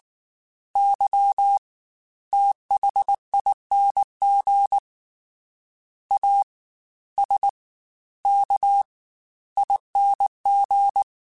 Morse code, heard as a keyed tone, about 19 wpm, 800 Hz. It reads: Y THING A S K ING
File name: Special INGs-mp3s / Special INGs-mp3s / Fast ing @18